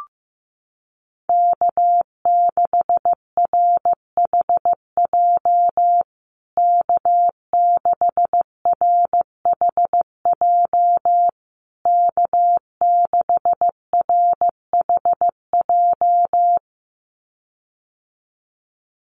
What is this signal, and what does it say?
Morse code, heard as a keyed tone, about 15 wpm, 700 Hz. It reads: K6RHJ K6RHJ K6RHJ
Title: USA Call Signs Repeated 3x 15wpm